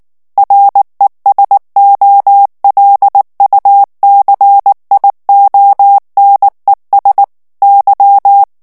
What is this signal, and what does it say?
Morse code, heard as a keyed tone, about 19 wpm, 800 Hz. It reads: RESOLUCIONES Y
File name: morse mp3 / ESPAGNOL TEXTE EN CLAIR 2 VITESSE 18 MOTS MINUTES